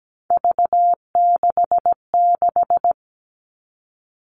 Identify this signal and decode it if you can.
Morse code, heard as a keyed tone, about 17 wpm, 700 Hz. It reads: V66
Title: All Cut Numbers - 3-Digit 17wpm